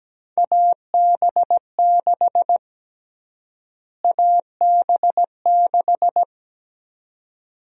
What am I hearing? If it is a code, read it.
Morse code, heard as a keyed tone, about 17 wpm, 700 Hz. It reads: AB6 AB6
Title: USA Call Sign Prefixes - Repeated 2x - 17wpm